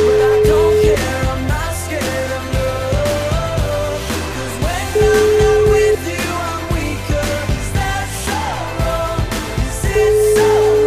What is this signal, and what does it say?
Ringback tone (Europe, 425 Hz), with noise.